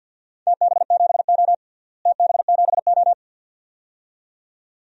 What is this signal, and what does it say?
Morse code, heard as a keyed tone, about 53 wpm, 700 Hz. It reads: TB7O TB7O
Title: DXCC-100 Call Signs - 4 Characters - Repeated 2x - 50wpm